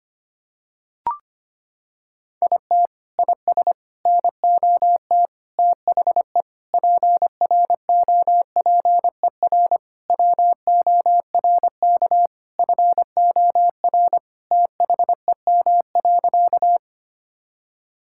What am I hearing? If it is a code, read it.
Morse code, heard as a keyed tone, about 25 wpm, 700 Hz. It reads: IT IS NOT THE PROPER WORK FOR THEM.
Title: Sentences from Top 1000 Words - Review Part 5 - 25wpm